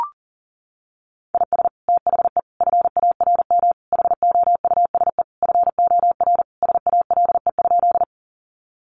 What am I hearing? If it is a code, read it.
Morse code, heard as a keyed tone, about 40 wpm, 700 Hz. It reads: IS THE FARM HOUSE FOR SALE?
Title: Sentences from Top 1400 Words 40wpm